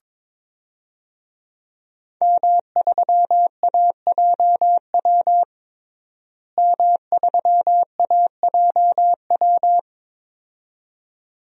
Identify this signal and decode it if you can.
Morse code, heard as a keyed tone, about 22 wpm, 700 Hz. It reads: M3AJW M3AJW
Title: DXCC-100 Call Signs - 5 Characters - Repeated 2x - 22wpm